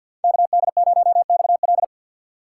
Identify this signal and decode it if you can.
Morse code, heard as a keyed tone, about 48 wpm, 700 Hz. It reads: KD0XL
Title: Speed Racing - USA 2x2 Call Signs 50wpm